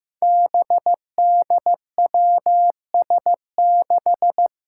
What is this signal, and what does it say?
Morse code, heard as a keyed tone, about 15 wpm, 700 Hz. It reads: BDWS6
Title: Five Challenging Letter-Number Combinations 15wpm